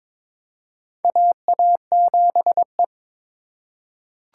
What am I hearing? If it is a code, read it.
Morse code, heard as a keyed tone, about 23 wpm, 700 Hz. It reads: AA7E
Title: USA Call Signs - 4 Characters - 4x Character Spacing - 22wpm